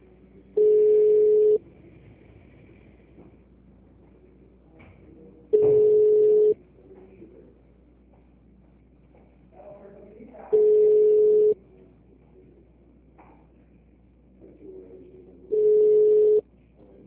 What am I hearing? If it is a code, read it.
Ringback tone (Europe, 425 Hz), with speech.
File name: out- -2007- - - .92